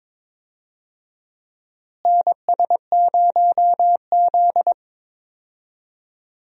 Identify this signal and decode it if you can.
Morse code, heard as a keyed tone, about 22 wpm, 700 Hz. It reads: NS0Z
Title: USA Call Signs - 4 Characters - Repeated 3x - 22wpm